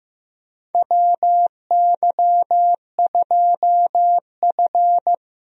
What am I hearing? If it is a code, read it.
Morse code, heard as a keyed tone, about 15 wpm, 700 Hz. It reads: WY2F
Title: USA Call Signs - 4 Characters - 15wpm